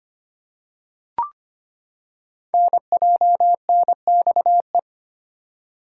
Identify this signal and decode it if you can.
Morse code, heard as a keyed tone, about 25 wpm, 700 Hz. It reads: NJNXE
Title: Code Groups - 5 Letters - 25wpm